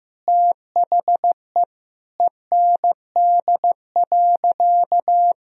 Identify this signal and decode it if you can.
Morse code, heard as a keyed tone, about 16 wpm, 700 Hz. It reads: THE END.
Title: Sentences from Top 1000 Words - Review Part 1 - 15wpm